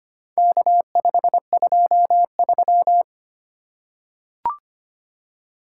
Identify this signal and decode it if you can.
Morse code, heard as a keyed tone, about 25 wpm, 700 Hz. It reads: K523
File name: practice / 25wpm / Four Letter-Number Combinations - 25wpm